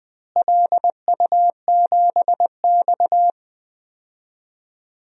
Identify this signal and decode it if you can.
Morse code, heard as a keyed tone, about 20 wpm, 700 Hz. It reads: LU7X
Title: DXCC-100 Call Signs - 4 Characters - Repeated 3x - 20wpm